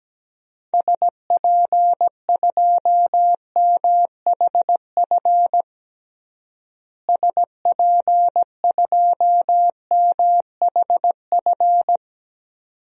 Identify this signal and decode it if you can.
Morse code, heard as a keyed tone, about 17 wpm, 700 Hz. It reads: SP2MHF SP2MHF
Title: DXCC-100 Call Signs - 6 Characters - Repeated 2x - 17wpm